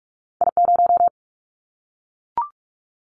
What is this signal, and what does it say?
Morse code, heard as a keyed tone, about 44 wpm, 700 Hz. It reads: I0
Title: Callsign Prefixes for Top 25 DXCC - Repeated 3x - 45wpm